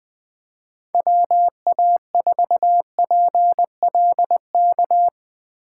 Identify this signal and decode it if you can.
Morse code, heard as a keyed tone, about 20 wpm, 700 Hz. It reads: WA4PLK
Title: Speed Racing - USA 2x3 Call Signs 17wpm